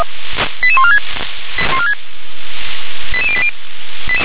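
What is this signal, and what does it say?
Five-tone selective call (ZVEI), heard as quick runs of five tones, tones of about 70 ms. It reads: 88146 88080
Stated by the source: ZVEI five-tone selective call received with an SDR.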